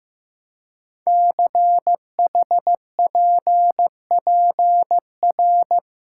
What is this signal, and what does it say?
Morse code, heard as a keyed tone, about 15 wpm, 700 Hz. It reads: CHPPR
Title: Code Groups - 5 Letters - 15wpm